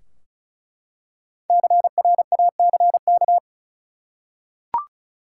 Morse code, heard as a keyed tone, about 36 wpm, 700 Hz. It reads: CRACK
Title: Five Letter Words - 35wpm